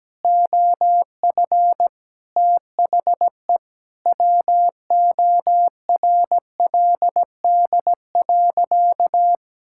Morse code, heard as a keyed tone, about 17 wpm, 700 Hz. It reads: OF THE WORLD.
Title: Sentences from Top 1000 Words - Review Part 5 - 17wpm